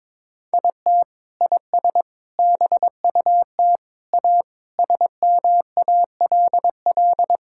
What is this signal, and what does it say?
Morse code, heard as a keyed tone, about 22 wpm, 700 Hz. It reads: IT IS BUT A SMALL
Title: Sentences from Top 1000 Words - Review Part 5 - 22wpm